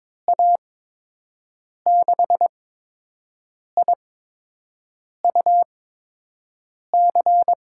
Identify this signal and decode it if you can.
Morse code, heard as a keyed tone, about 22 wpm, spaced as for 6 wpm, 700 Hz. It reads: A6IUC
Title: USA Call Signs - 6 Characters - 8x Character Spacing - 22wpm